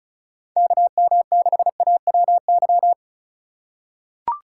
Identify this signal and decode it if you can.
Morse code, heard as a keyed tone, about 34 wpm, 700 Hz. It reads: KM6AWY
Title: USA Call Signs - 6 Characters - 2x Character Spacing - 35wpm